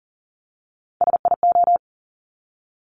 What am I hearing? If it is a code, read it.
Morse code, heard as a keyed tone, about 41 wpm, 690 Hz. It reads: SIO
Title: Top Three Letter Combinations in Words 40wpm